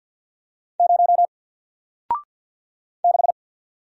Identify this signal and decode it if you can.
Morse code, heard as a keyed tone, about 50 wpm, 700 Hz. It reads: 0 6
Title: Single Character (Letter:Number:Punctuation:Prosign) 50wpm